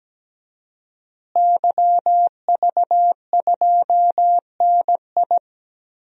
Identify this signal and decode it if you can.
Morse code, heard as a keyed tone, about 17 wpm, 700 Hz. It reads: YV2NI
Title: DXCC-100 Call Signs - 5 Characters - Repeated 2x - 17wpm